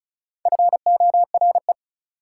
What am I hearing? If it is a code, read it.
Morse code, heard as a keyed tone, about 36 wpm, 700 Hz. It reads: FORE
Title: Prefixes and Suffixes 35wpm